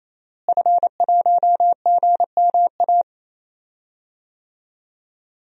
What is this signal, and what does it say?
Morse code, heard as a keyed tone, about 28 wpm, 700 Hz. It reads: F1GMA
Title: DXCC-100 Call Signs - 5 Characters - Repeated 3x - 28wpm